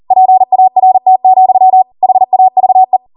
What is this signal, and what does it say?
Morse code, heard as a keyed tone, about 40 wpm, 750 Hz. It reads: PART, HAVE